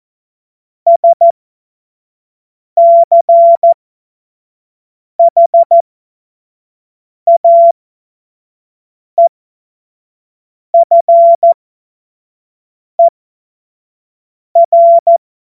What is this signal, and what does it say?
Morse code, heard as a keyed tone, about 14 wpm, spaced as for 5 wpm, 680 Hz. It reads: SCHAEFER
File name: Morsekurs / Witze / Witz 18. April 2023 25 F70